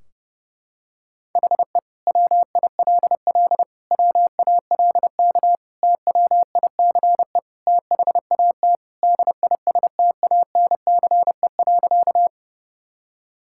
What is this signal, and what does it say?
Morse code, heard as a keyed tone, about 32 wpm, 700 Hz. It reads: HE WILL WALK TWICE THAT DISTANCE.
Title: Sentences from Top 1600 Words 30wpm